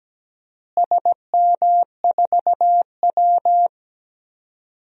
Morse code, heard as a keyed tone, about 17 wpm, 700 Hz. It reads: SM4W
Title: DXCC-100 Call Signs - 4 Characters - Repeated 2x - 17wpm